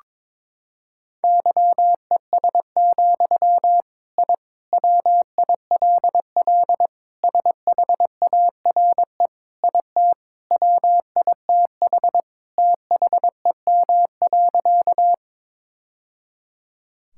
Morse code, heard as a keyed tone, about 22 wpm, 700 Hz. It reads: YES, I WILL SHARE IT WITH THEM.
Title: Sentences from Top 1000 Words - Review Part 6 - 22wpm